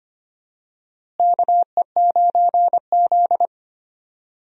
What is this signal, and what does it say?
Morse code, heard as a keyed tone, about 25 wpm, 700 Hz. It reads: KE9Z
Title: USA Call Signs - 4 Characters - Repeated 2x - 25wpm